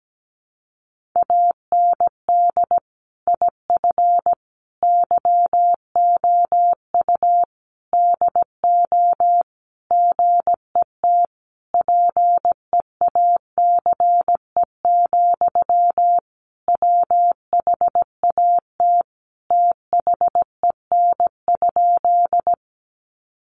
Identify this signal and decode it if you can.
Morse code, heard as a keyed tone, about 17 wpm, 700 Hz. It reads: AND IF YOU DO GET PEACE, WHAT THEN?
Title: Sentences from Top 1500 Words 17wpm